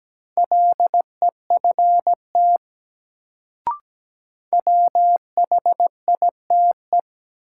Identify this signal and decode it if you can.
Morse code, heard as a keyed tone, about 17 wpm, 700 Hz. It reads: LEFT WHITE
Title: Sentences from Top 300 Words - New Word Drill - 17wpm